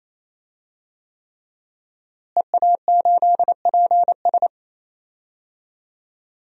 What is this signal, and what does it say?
Morse code, heard as a keyed tone, about 28 wpm, 700 Hz. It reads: EA8PS
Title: DXCC-100 Call Signs - 5 Characters - Repeated 3x - 28wpm